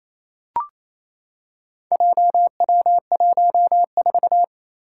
Morse code, heard as a keyed tone, about 27 wpm, 700 Hz. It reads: JW14
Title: Four Challenging Letter-Number Combinations 28wpm